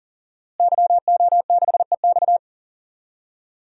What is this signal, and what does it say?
Morse code, heard as a keyed tone, about 37 wpm, 700 Hz. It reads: YO6EX
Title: DXCC-100 Call Signs - 5 Characters - Repeated 3x - 40wpm